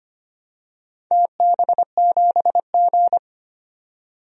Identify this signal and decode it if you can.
Morse code, heard as a keyed tone, about 25 wpm, 700 Hz. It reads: TB7G
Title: DXCC-100 Call Signs - 4 Characters - 25wpm